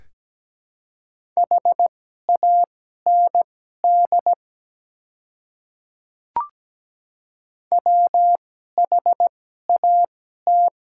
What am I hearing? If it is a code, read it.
Morse code, heard as a keyed tone, about 17 wpm, spaced as for 12 wpm, 700 Hz. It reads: HAND WHAT
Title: 2x Character Spacing - Top 300 Words - 17wpm